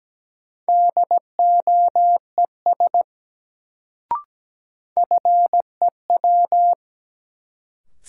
Morse code, heard as a keyed tone, about 17 wpm, 700 Hz. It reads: DOES FEW
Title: Sets of 2 Words - New Word Drill - 17wpm